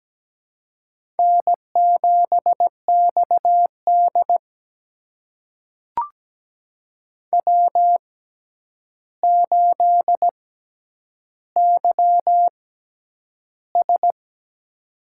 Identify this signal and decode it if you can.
Morse code, heard as a keyed tone, about 17 wpm, spaced as for 9 wpm, 700 Hz. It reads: N7XD W8YS
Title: USA Call Signs - 4 Characters - 6x Character Spacing - 17wpm